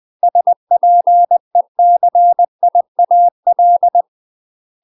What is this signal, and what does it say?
Morse code, heard as a keyed tone, about 20 wpm, 700 Hz. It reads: SPECIAL
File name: UK MASTER QSOs TRAINING/20WPM